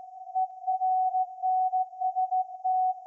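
Morse code, heard as a keyed tone, about 16 wpm, 750 Hz. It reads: ERNST
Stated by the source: Morse code from SAQ, received at a station in Putignano, Apulia, Italy, on VLF.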